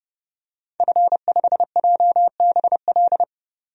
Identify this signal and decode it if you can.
Morse code, heard as a keyed tone, about 30 wpm, 700 Hz. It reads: F5JBL
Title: DXCC-100 Call Signs - 5 Characters - 30wpm